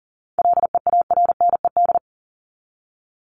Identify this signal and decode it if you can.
Morse code, heard as a keyed tone, about 40 wpm, 700 Hz. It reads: LEARNED
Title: ED Suffix - Top 100 Words with Base Words 40wpm